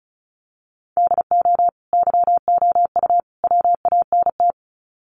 Morse code, heard as a keyed tone, about 34 wpm, 700 Hz. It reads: DO YOU WANT
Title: Sets of 3 Words 35wpm